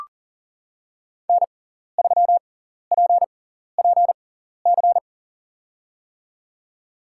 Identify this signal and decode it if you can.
Morse code, heard as a keyed tone, about 40 wpm, spaced as for 14 wpm, 700 Hz. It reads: N3PPC
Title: USA Call Signs - 5 Characters - 6x Character Spacing - 40wpm